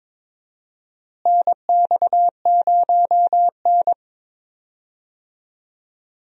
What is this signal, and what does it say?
Morse code, heard as a keyed tone, about 22 wpm, 700 Hz. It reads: NX0N
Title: USA Call Signs - 4 Characters - Repeated 2x - 22wpm